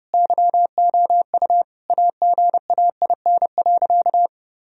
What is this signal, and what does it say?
Morse code, heard as a keyed tone, about 30 wpm, 700 Hz. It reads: YOU AGAIN.
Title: Sentences from Top 1000 Words - Review Part 1 - 30wpm